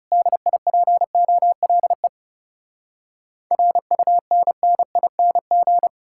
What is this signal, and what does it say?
Morse code, heard as a keyed tone, about 40 wpm, 700 Hz. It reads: DIPOLE RUNNING